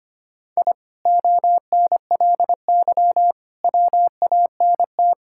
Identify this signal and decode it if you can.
Morse code, heard as a keyed tone, about 25 wpm, 700 Hz. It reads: I ONLY WANT
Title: Sentences from Top 1000 Words - Review Part 6 - 25wpm